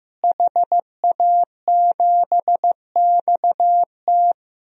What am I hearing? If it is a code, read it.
Morse code, heard as a keyed tone, about 15 wpm, 700 Hz. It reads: HA7XT
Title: DXCC-100 Call Signs - 5 Characters - Repeated 2x - 15wpm